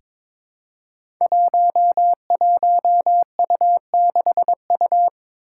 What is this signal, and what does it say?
Morse code, heard as a keyed tone, about 22 wpm, 700 Hz. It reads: 11U6U
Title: Five Challenging Letter-Number Combinations 22wpm